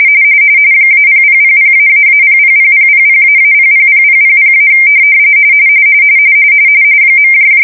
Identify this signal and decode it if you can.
RTTY, about 45.45 baud. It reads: -0000---000000--000000- .
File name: fsk-ascii170-pix / shell